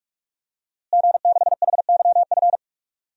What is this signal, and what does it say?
Morse code, heard as a keyed tone, about 43 wpm, 700 Hz. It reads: G6HYF
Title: DXCC-100 Call Signs - 5 Characters - Repeated 3x - 45wpm